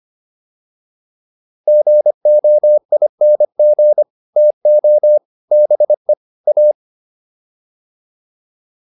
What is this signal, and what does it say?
Morse code, heard as a keyed tone, about 25 wpm, 600 Hz. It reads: GOING TO BE A